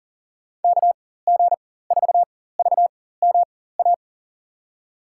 Morse code, heard as a keyed tone, about 39 wpm, spaced as for 18 wpm, 700 Hz. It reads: KG4VMA
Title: USA Call Signs - 6 Characters - 4x Character Spacing - 40wpm